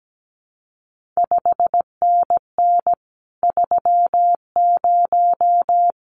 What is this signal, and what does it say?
Morse code, heard as a keyed tone, about 17 wpm, 700 Hz. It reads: 5NN 30
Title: CQ WW DX Contest Exchange Encore - 17wpm